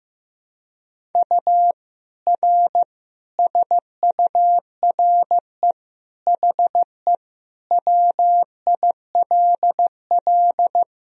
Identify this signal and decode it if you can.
Morse code, heard as a keyed tone, about 15 wpm, 700 Hz. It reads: U R SURE HE WILL
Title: Sentences from Top 300 Words and Abbreviations 15wpm